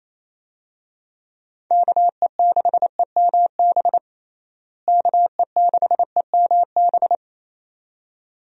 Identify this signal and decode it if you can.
Morse code, heard as a keyed tone, about 28 wpm, 700 Hz. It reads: KE6EMB KE6EMB
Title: USA Call Signs - 6 Characters - Repeated 2x - 28wpm